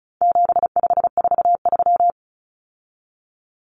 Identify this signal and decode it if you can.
Morse code, heard as a keyed tone, about 34 wpm, 700 Hz. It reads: 7543
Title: Common Cut Numbers - 4-Digit 35wpm